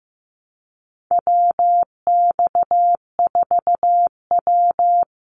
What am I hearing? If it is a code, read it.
Morse code, heard as a keyed tone, about 15 wpm, 700 Hz. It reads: WX4W
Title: Call Signs Repeated 3x - 15wpm